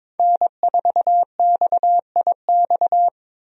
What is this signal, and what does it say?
Morse code, heard as a keyed tone, about 22 wpm, 700 Hz. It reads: N4XIX
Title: USA Call Signs - 5 Characters - 22wpm